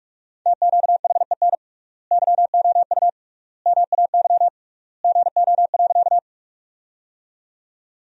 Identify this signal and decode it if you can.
Morse code, heard as a keyed tone, about 46 wpm, 700 Hz. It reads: TQHEN YOU MAY GO.
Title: 3x Word-Spacing - Sentences from Top 100 Words - 45wpm